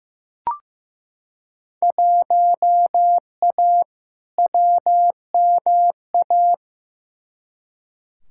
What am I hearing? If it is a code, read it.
Morse code, heard as a keyed tone, about 15 wpm, 700 Hz. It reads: 1A WMA
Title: Field Day Exchanges - 15wpm